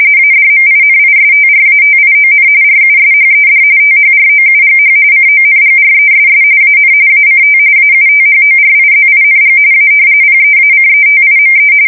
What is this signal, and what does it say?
RTTY, about 45.45 baud. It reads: ADMINISTRATION RUSHED THROUGH CONGRESS AN ACT AUTHORIZING A PEACETIME
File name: fsk170 / usnavy1